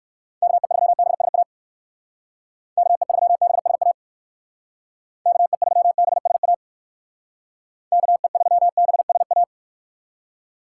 Morse code, heard as a keyed tone, about 58 wpm, 700 Hz. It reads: KE3BSA KE3BSA KE3BSA KE3BSA